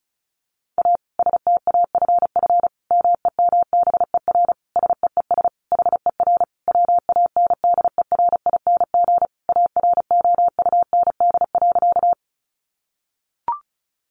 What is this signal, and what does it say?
Morse code, heard as a keyed tone, about 34 wpm, 700 Hz. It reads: A STAFF MEMBER SEES HER WANDERING AROUND.
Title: Additional Easy Phrases 35wpm